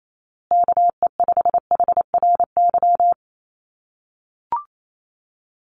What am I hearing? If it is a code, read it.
Morse code, heard as a keyed tone, about 28 wpm, 700 Hz. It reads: KE5HRY
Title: USA Call Signs Repeated 2x 28wpm